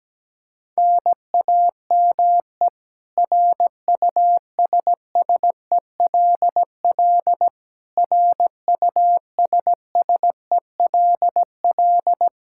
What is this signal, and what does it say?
Morse code, heard as a keyed tone, about 17 wpm, 700 Hz. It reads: NAME RUSSELL RUSSELL
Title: K1USN SST QSO Practice 17wpm